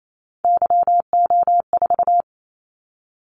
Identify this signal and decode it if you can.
Morse code, heard as a keyed tone, about 28 wpm, 700 Hz. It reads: YO4
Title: Callsign Prefixes for Top 25 DXCC - Repeated 3x - 28wpm